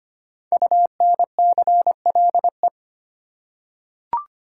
Morse code, heard as a keyed tone, about 25 wpm, 700 Hz. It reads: UNCLE
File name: 5 Letter Words - First 3 Letters Given - 25wpm